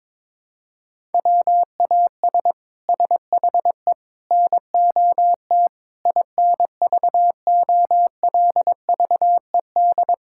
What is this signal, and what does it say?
Morse code, heard as a keyed tone, about 23 wpm, 700 Hz. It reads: WAS SHE NOT INVOLVED
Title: Sentences from Top 1000 Words - Review Part 5 - 22wpm